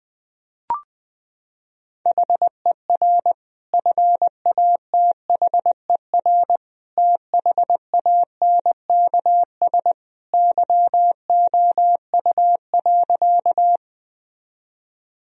Morse code, heard as a keyed tone, about 20 wpm, 700 Hz. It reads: HER FATHER THANKS YOU.
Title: Sentences from Top 1000 Words - Review Part 3 - 20wpm